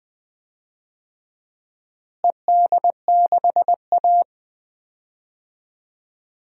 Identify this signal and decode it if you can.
Morse code, heard as a keyed tone, about 20 wpm, 700 Hz. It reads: ED6A
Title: DXCC-100 Call Signs - 4 Characters - Repeated 3x - 20wpm